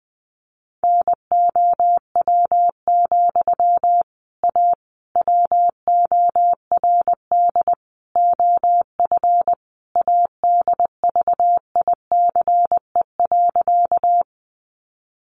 Morse code, heard as a keyed tone, about 20 wpm, 700 Hz. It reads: NOW, A WORD OF ADVICE.
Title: Sentences from Top 1100 Words 20wpm